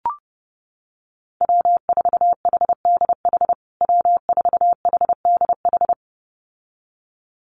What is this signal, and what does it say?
Morse code, heard as a keyed tone, about 30 wpm, 700 Hz. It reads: W4HDH W4HDH
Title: USA Call Signs Repeated 2x 30wpm